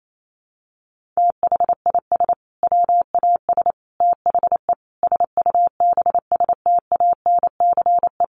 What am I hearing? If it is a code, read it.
Morse code, heard as a keyed tone, about 28 wpm, 700 Hz. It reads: THIS WAS THE SUBSTANCE
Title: Sentences from Top 1500 Words 28wpm